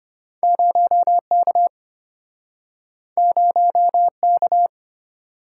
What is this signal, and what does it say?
Morse code, heard as a keyed tone, about 33 wpm, 690 Hz. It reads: 0K 0K